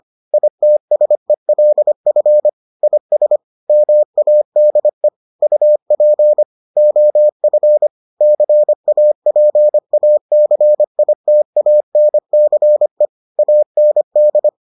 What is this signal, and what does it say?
Morse code, heard as a keyed tone, about 25 wpm, 600 Hz. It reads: ITSELF IS MADE UP OF CAPACITANCE AND